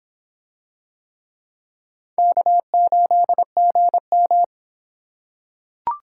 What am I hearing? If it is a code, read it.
Morse code, heard as a keyed tone, about 23 wpm, 700 Hz. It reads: K8GM